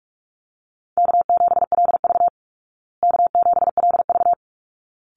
Morse code, heard as a keyed tone, about 44 wpm, 700 Hz. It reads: K7LV K7LV
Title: Easy Call Signs Repeated 2x - 45wpm